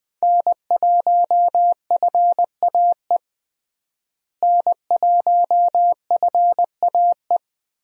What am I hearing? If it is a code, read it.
Morse code, heard as a keyed tone, about 20 wpm, 700 Hz. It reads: N1FAE N1FAE
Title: USA Call Signs - 5 Characters - Repeated 3x - 20wpm